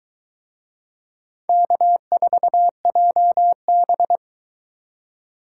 Morse code, heard as a keyed tone, about 22 wpm, 700 Hz. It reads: K4JB